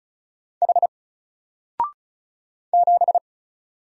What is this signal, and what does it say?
Morse code, heard as a keyed tone, about 33 wpm, 700 Hz. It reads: H 7
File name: Single-Letter-Number-35wpm